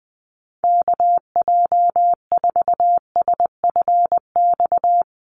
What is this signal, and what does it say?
Morse code, heard as a keyed tone, about 20 wpm, 700 Hz. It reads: KJ4SFX
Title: USA Call Signs 20wpm